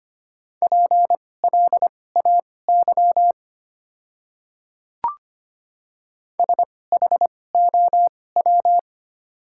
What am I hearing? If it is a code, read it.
Morse code, heard as a keyed tone, about 25 wpm, spaced as for 18 wpm, 690 Hz. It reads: PLAY SHOW
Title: 2x Character Spacing - Top 200 Words - 25wpm